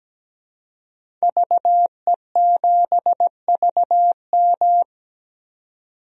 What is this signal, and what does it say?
Morse code, heard as a keyed tone, about 17 wpm, 700 Hz. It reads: VE7VM
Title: DXCC-100 Call Signs - 5 Characters - 17wpm